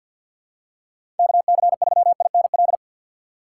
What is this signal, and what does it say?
Morse code, heard as a keyed tone, about 49 wpm, 700 Hz. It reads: KC3INL
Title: USA Call Signs - 6 Characters - 4x Character Spacing - 50wpm